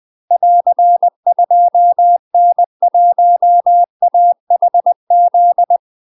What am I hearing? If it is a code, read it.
Morse code, heard as a keyed tone, about 20 wpm, 700 Hz. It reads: <AR> 2N1AHZ
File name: UK MASTER QSOs TRAINING/20WPM